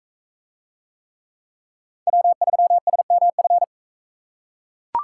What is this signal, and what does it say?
Morse code, heard as a keyed tone, about 39 wpm, 700 Hz. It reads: W3SMF